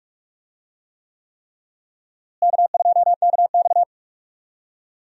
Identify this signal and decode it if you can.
Morse code, heard as a keyed tone, about 47 wpm, 700 Hz. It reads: K2KX
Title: USA Call Signs - 4 Characters - Repeated 3x - 45wpm